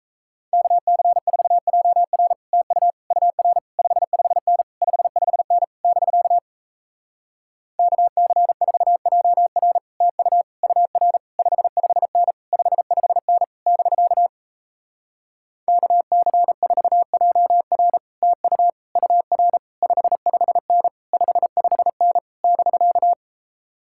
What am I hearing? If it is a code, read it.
Morse code, heard as a keyed tone, about 38 wpm, 700 Hz. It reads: KC4JR TU UR 55N 55N <BK> KC4JR TU UR 55N 55N <BK> KC4JR TU UR 55N 55N <BK>